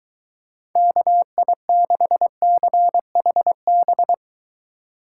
Morse code, heard as a keyed tone, about 23 wpm, 700 Hz. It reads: KI6CHB